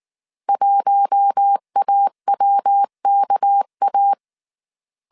Morse code, heard as a keyed tone, about 19 wpm, 800 Hz. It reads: 1AWKA
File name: CW / DET / 1 6 1500C S A 18wpm 2255